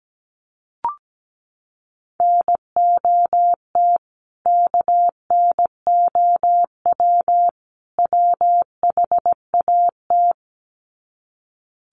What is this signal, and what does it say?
Morse code, heard as a keyed tone, about 17 wpm, 700 Hz. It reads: NOT KNOW WHAT
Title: Sets of 3 Words 17wpm